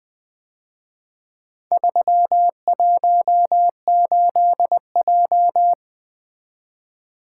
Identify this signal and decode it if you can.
Morse code, heard as a keyed tone, about 24 wpm, 700 Hz. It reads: 318J
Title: Speed-Racing - Four Letter-Number Combinations - 20wpm